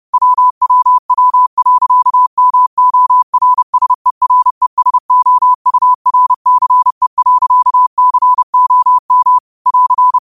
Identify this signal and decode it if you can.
Morse code, heard as a keyed tone, about 30 wpm, 1000 Hz. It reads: WWW.MORSERESOURCE.COM <AR>